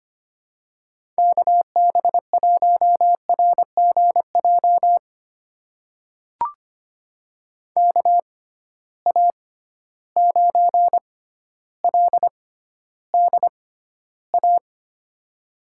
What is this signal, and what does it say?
Morse code, heard as a keyed tone, about 25 wpm, spaced as for 13 wpm, 700 Hz. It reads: KB1RGJ KA9LDA
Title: USA Call Signs - 6 Characters - 6x Character Spacing - 25wpm